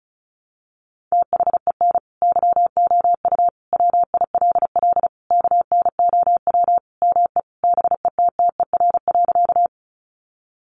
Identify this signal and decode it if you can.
Morse code, heard as a keyed tone, about 34 wpm, 700 Hz. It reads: THEN YOU WILL KNOW ME BETTER.
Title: Sentences from Top 400 Words 35wpm